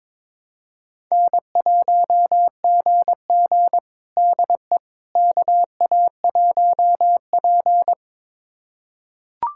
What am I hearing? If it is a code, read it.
Morse code, heard as a keyed tone, about 22 wpm, 700 Hz. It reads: N1GG DE KA1P
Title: USA Callsign DE Callsign - 4 Characters - 22wpm